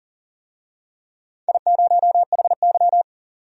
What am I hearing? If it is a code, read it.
Morse code, heard as a keyed tone, about 39 wpm, 700 Hz. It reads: I0HY
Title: DXCC-100 Call Signs - 4 Characters - 40wpm